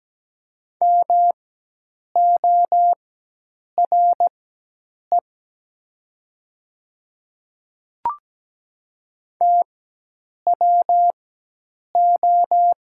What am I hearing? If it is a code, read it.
Morse code, heard as a keyed tone, about 17 wpm, spaced as for 8 wpm, 700 Hz. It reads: MORE TWO
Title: 4x Character Spacing - Top 100 Words - 17wpm